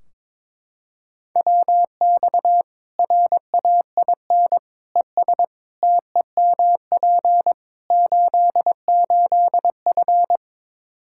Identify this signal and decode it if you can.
Morse code, heard as a keyed tone, about 22 wpm, 700 Hz. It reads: WX RAIN ES TEMP 88F
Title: Practice QSO - Exchange 2 - Info - 22wpm